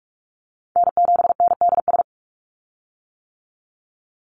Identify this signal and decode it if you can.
Morse code, heard as a keyed tone, about 44 wpm, 700 Hz. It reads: N7NDS
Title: USA Call Signs 45wpm